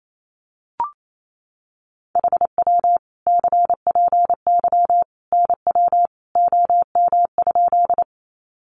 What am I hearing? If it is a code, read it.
Morse code, heard as a keyed tone, about 28 wpm, 700 Hz. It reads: HW CPY NW OM?
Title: Common QSO Phrases 28wpm